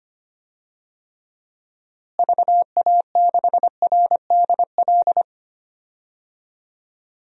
Morse code, heard as a keyed tone, about 25 wpm, 700 Hz. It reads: VA6RDL
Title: DXCC-100 Call Signs - 6 Characters - Repeated 3x - 25wpm